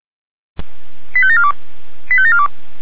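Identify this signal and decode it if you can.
Five-tone selective call (ZVEI), heard as quick runs of five tones, tones of about 70 ms. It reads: 85742 85742
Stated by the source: A ZVEI five-tone selective call, code 89262, received with an SDR by the Orlamünde fire brigade (feuerwehr-orlamuende).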